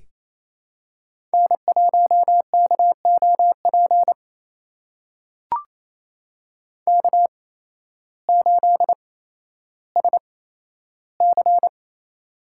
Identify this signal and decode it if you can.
Morse code, heard as a keyed tone, about 28 wpm, spaced as for 13 wpm, 700 Hz. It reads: N1KOP K8SC
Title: USA Call Signs - 8x Character Spacing - 28wpm